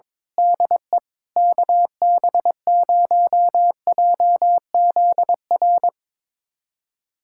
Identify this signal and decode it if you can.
Morse code, heard as a keyed tone, about 22 wpm, 700 Hz. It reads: DE KB0JZR
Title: USA Callsign DE Callsign - 6 Characters - 22wpm